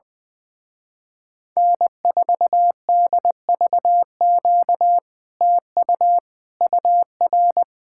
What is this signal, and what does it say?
Morse code, heard as a keyed tone, about 20 wpm, 700 Hz. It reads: N4DVQ TU UR
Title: Speed-Racing Common QSO Phrases - 20wpm